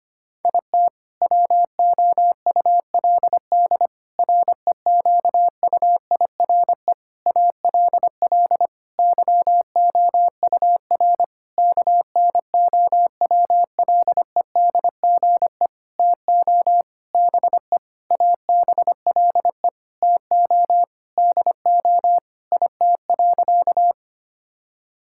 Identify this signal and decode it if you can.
Morse code, heard as a keyed tone, about 25 wpm, 700 Hz. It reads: IT WOULD REQUIRE ALL YOUR KNOWLEDGE TO BE ABLE TO DO IT.
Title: Sentences from Top 1000 Words - Review Part 6 - 25wpm